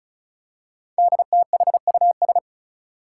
Morse code, heard as a keyed tone, about 36 wpm, 700 Hz. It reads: DTHUS
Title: Five Challenging Letter-Number Combinations 35wpm